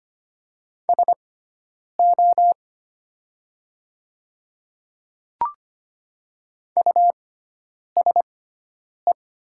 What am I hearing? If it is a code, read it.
Morse code, heard as a keyed tone, about 25 wpm, spaced as for 9 wpm, 700 Hz. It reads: SO USE
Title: 6x Character Spacing - Top 200 Words - 25wpm